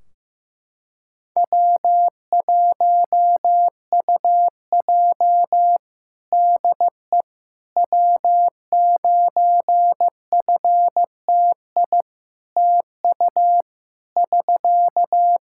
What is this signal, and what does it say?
Morse code, heard as a keyed tone, about 16 wpm, 700 Hz. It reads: W1UJ DE W9FTI TU <SK>
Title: Practice QSO - Exchange 4 - Ending - 15wpm